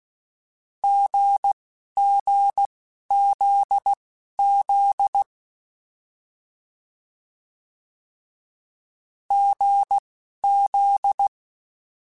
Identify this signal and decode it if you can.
Morse code, heard as a keyed tone, about 16 wpm, spaced as for 11 wpm, 800 Hz. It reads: GGZZ GZ
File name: Difficult Combos